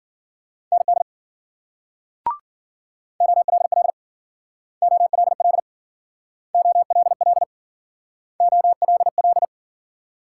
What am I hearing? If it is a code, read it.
Morse code, heard as a keyed tone, about 50 wpm, 700 Hz. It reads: ND OLL OLL OLL OLL
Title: Speed-Racing Multi-letter Phonemes - 40wpm